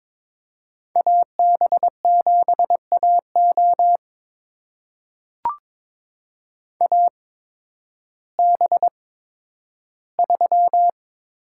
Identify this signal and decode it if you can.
Morse code, heard as a keyed tone, about 22 wpm, spaced as for 12 wpm, 700 Hz. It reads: AB7AO AB3
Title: USA Call Signs - 5 Characters - 8x Character Spacing - 22wpm